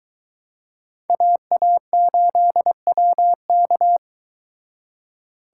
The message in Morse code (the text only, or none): AA8WK